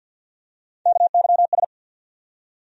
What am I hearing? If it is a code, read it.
Morse code, heard as a keyed tone, about 47 wpm, 700 Hz. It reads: KYS
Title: USA Call Sign Postfixes - 3 Letters 50wpm